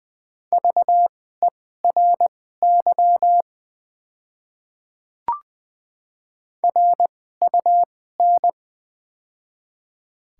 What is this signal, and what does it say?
Morse code, heard as a keyed tone, about 20 wpm, spaced as for 15 wpm, 700 Hz. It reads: VERY RUN
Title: 2x Character Spacing - Top 300 Words - 20wpm